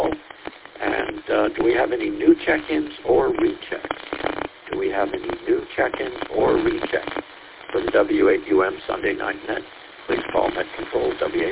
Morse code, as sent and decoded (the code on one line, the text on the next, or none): none